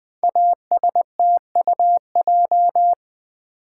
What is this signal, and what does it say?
Morse code, heard as a keyed tone, about 20 wpm, 700 Hz. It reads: ASTUJ
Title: Code Groups - 5 Letters - 20wpm